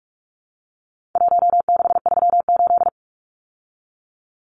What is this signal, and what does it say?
Morse code, heard as a keyed tone, about 44 wpm, 700 Hz. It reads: 1638
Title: Numbers - Contest Style - 4-Digit 45wpm